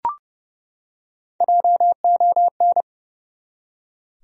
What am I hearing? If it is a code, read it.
Morse code, heard as a keyed tone, about 30 wpm, 700 Hz. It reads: JON
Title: USA Common Names - Max 3 Letters - 30wpm